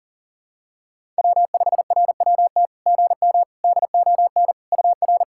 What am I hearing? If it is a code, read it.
Morse code, heard as a keyed tone, about 40 wpm, 690 Hz. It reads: W5RWT GM DON UR
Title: SOTA QSO Practice 40wpm